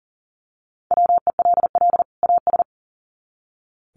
Morse code, heard as a keyed tone, about 40 wpm, 700 Hz. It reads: WELL AS
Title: Sets of 2 Words 40wpm